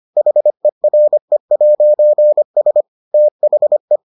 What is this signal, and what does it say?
Morse code, heard as a keyed tone, about 25 wpm, 600 Hz. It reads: HERE'S THE